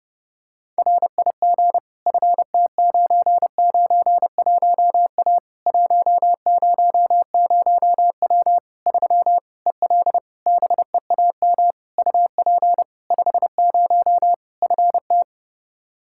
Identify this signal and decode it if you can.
Morse code, heard as a keyed tone, about 30 wpm, 700 Hz. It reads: RIG FT991A 100W 3 EL BEAM UP 50 FT
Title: K1USN SST QSO Practice 30wpm